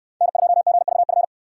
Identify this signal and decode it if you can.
Morse code, heard as a keyed tone, about 68 wpm, 700 Hz. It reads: N2GVF